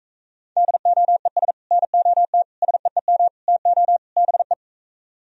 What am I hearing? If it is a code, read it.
Morse code, heard as a keyed tone, about 36 wpm, 700 Hz. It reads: DOES NOT SEEM TO BE